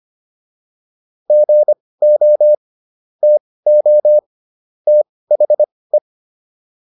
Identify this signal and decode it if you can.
Morse code, heard as a keyed tone, about 25 wpm, spaced as for 18 wpm, 600 Hz. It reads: GO TO THE